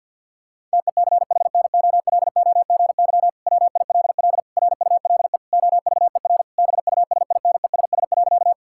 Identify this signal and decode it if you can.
Morse code, heard as a keyed tone, about 50 wpm, 700 Hz. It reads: TECHNOLOGY WILL RULE OVER BUSINESS.
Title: Sentences from Top 1000 Words - Review Part 5 - 50wpm